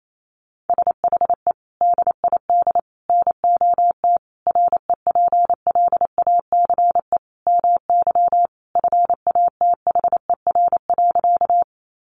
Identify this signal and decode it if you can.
Morse code, heard as a keyed tone, about 28 wpm, 700 Hz. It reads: SHE DID NOT REPLACE MY FATHER.
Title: Sentences from Top 1400 Words 28wpm